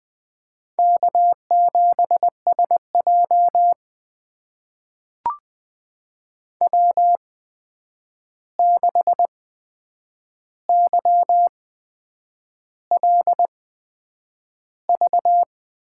Morse code, heard as a keyed tone, about 20 wpm, spaced as for 8 wpm, 700 Hz. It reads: K7SJ W6YLV
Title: USA Call Signs - 8x Character Spacing - 20wpm